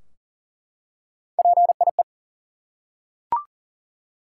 Morse code, heard as a keyed tone, about 38 wpm, 700 Hz. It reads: PIE
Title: Three Letter Words - 40wpm